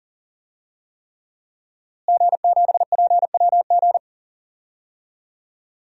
Morse code, heard as a keyed tone, about 37 wpm, 700 Hz. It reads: G7PWG
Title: DXCC-100 Call Signs - 5 Characters - Repeated 3x - 40wpm